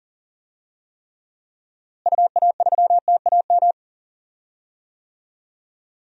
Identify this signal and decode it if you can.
Morse code, heard as a keyed tone, about 40 wpm, 700 Hz. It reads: UA3TAM
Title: DXCC-100 Call Signs - 6 Characters - Repeated 2x - 40wpm